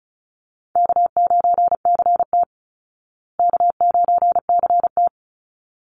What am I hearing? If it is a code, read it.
Morse code, heard as a keyed tone, about 34 wpm, 700 Hz. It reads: K9CT K9CT
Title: Call Signs Repeated 3x - 35wpm